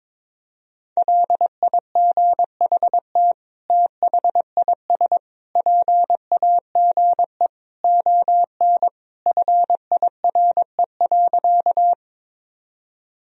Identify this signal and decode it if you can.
Morse code, heard as a keyed tone, about 22 wpm, 700 Hz. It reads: LIGHT THIS PAGE ON FIRE.
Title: Sentences from Top 1000 Words - Review Part 1 - 22wpm